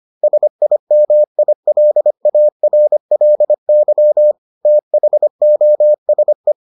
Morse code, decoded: SIMILARLY THOSE

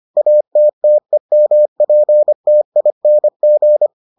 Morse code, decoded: ATTEMPTING

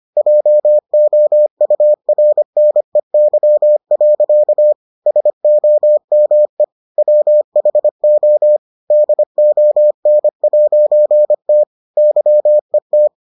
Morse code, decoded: JOURNEY. SOME WHO DON'T YET

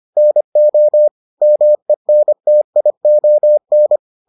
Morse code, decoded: NO MENTION

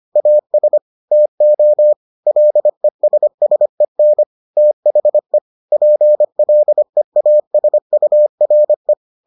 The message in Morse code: AS TO LESSEN THE PLEASURE

